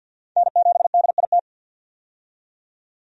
Morse code, heard as a keyed tone, about 47 wpm, 700 Hz. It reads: N7DIT
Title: USA Call Signs - 5 Characters - 50wpm